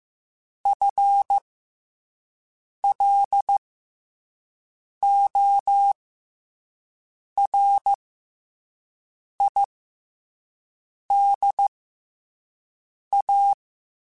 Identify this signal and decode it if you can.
Morse code, heard as a keyed tone, about 15 wpm, spaced as for 5 wpm, 800 Hz. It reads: FLORIDA
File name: FL Runs / FL Runs mp3s / FL Runs @5